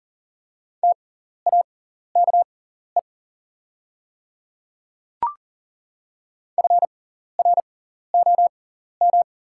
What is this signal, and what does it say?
Morse code, heard as a keyed tone, about 40 wpm, spaced as for 14 wpm, 690 Hz. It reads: TAKE FROM
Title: 6x Character Spacing - Top 300 Words - 40wpm